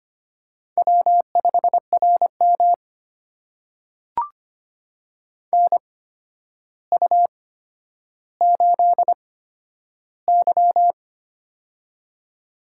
Morse code, heard as a keyed tone, about 25 wpm, spaced as for 11 wpm, 700 Hz. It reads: W5RM NU8Y
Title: USA Call Signs - 4 Characters - 8x Character Spacing - 25wpm